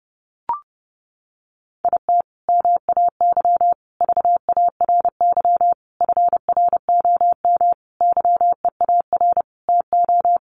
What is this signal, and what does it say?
Morse code, heard as a keyed tone, about 28 wpm, 700 Hz. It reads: IT MAY VARY FROM YEAR TO
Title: Sentences from Top 1200 Words 30wpm